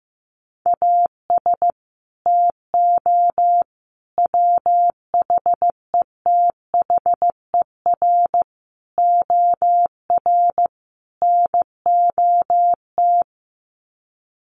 Morse code, heard as a keyed tone, about 15 wpm, 700 Hz. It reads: AS TO WHETHER OR NOT
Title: Sets of 5 Words Encore 15wpm